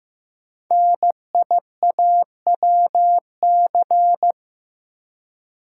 Morse code, heard as a keyed tone, about 15 wpm, 700 Hz. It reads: NIAWC
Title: Code Groups - 5 Letters - 15wpm